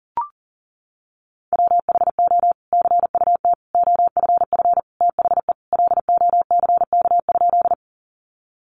Morse code, heard as a keyed tone, about 40 wpm, 700 Hz. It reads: WHO CUT OFF THE LOCK?
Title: Sentences from Top 1300 Words 40wpm